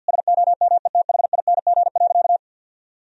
Morse code, heard as a keyed tone, about 50 wpm, 700 Hz. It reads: SOMETHING.
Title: Sentences from Top 1000 Words - Review Part 3 - 50wpm